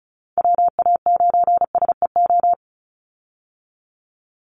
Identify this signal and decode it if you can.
Morse code, heard as a keyed tone, about 34 wpm, 700 Hz. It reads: WA9SEO
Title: Call Signs Repeated 3x - 35wpm